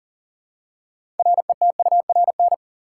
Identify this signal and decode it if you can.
Morse code, heard as a keyed tone, about 40 wpm, 700 Hz. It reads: RETURN
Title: Six Letter Words - 40wpm